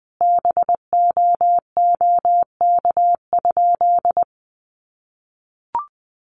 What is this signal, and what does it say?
Morse code, heard as a keyed tone, about 20 wpm, 700 Hz. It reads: BOOK?
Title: Sentences from Top 1400 Words 20wpm